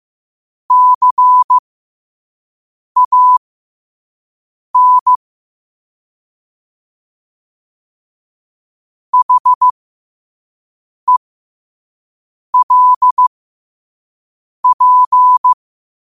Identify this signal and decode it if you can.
Morse code, heard as a keyed tone, about 15 wpm, spaced as for 5 wpm, 1000 Hz. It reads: CAN HELP
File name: qotdfiles / 5wpm